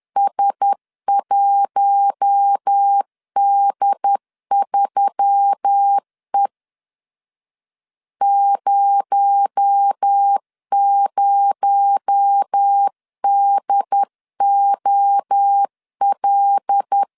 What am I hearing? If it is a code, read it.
Morse code, heard as a keyed tone, about 10 wpm, 800 Hz. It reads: S1D3E 00DOL
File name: CW / DET / 1 6 1500C S C 10wpm 4252